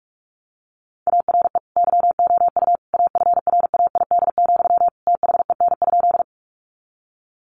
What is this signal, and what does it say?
Morse code, heard as a keyed tone, about 44 wpm, 700 Hz. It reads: ARE YOU AFRAID, THEN?
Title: Sentences from Top 1500 Words 45wpm